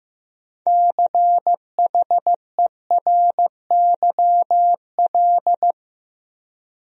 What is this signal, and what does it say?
Morse code, heard as a keyed tone, about 15 wpm, 700 Hz. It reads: CHERYL
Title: USA Common Names - 15wpm